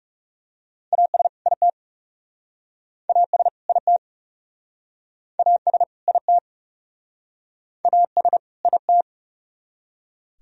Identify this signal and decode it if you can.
Morse code, heard as a keyed tone, about 40 wpm, 700 Hz. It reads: AS IT AS IT AS IT AS IT